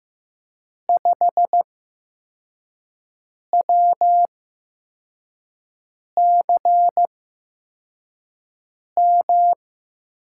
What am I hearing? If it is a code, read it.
Morse code, heard as a keyed tone, about 15 wpm, spaced as for 4 wpm, 700 Hz. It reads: 5WCM
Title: USA Call Signs - 6 Characters - 8x Character Spacing - 15wpm